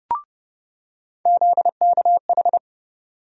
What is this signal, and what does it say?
Morse code, heard as a keyed tone, about 32 wpm, 700 Hz. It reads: ZKH
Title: USA Call Sign Postfixes - 3 Letters 30wpm